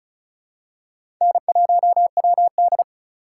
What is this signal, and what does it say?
Morse code, heard as a keyed tone, about 34 wpm, 700 Hz. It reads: N1WD
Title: USA Call Signs - 4 Characters - 4x Character Spacing - 35wpm